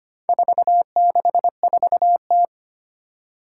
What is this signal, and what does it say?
Morse code, heard as a keyed tone, about 25 wpm, 700 Hz. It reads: 464T
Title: Four Challenging Letter-Number Combinations 25wpm